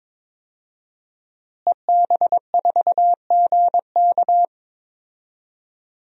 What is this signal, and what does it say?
Morse code, heard as a keyed tone, about 22 wpm, 700 Hz. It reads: EB4GK
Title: DXCC-100 Call Signs - 5 Characters - Repeated 2x - 22wpm